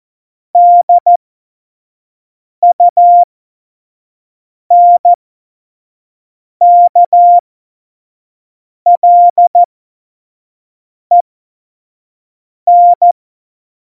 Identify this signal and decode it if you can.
Morse code, heard as a keyed tone, about 14 wpm, spaced as for 5 wpm, 700 Hz. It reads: DUNKLEN
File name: Morsekurs / Witze / Witz 26. Februar 2019, 25 F70